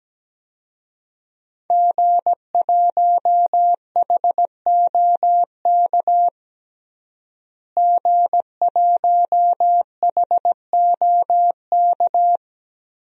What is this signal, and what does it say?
Morse code, heard as a keyed tone, about 17 wpm, 700 Hz. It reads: G1HOK G1HOK
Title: DXCC-100 Call Signs - 5 Characters - Repeated 3x - 17wpm